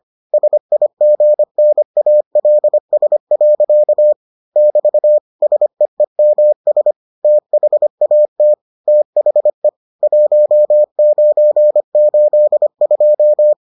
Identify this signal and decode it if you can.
Morse code, heard as a keyed tone, about 25 wpm, 600 Hz. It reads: SIGNALS. <BT> SEEMS THAT THE 1982